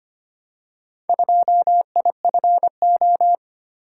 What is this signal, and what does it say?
Morse code, heard as a keyed tone, about 25 wpm, 700 Hz. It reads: 2IFO
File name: practice / 25wpm / Four Letter-Number Combinations - 25wpm